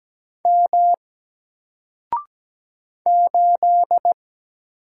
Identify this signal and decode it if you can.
Morse code, heard as a keyed tone, about 17 wpm, 700 Hz. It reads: M 8
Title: Single Character (Letter:Number:Punctuation:Prosign) Alt 17wpm